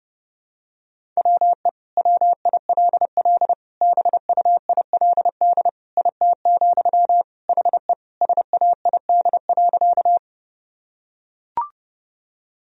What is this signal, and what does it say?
Morse code, heard as a keyed tone, about 32 wpm, 700 Hz. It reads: WE WILL BUILD IT, HE SAID.
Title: Sentences from Top 1000 Words - Review Part 1 - 30wpm